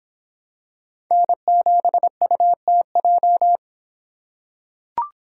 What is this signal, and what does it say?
Morse code, heard as a keyed tone, about 24 wpm, 700 Hz. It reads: N7UTJ